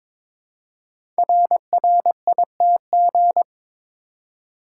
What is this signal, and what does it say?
Morse code, heard as a keyed tone, about 23 wpm, 690 Hz. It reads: RRITG
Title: Code Groups - 5 Letter:Numbers - 22wpm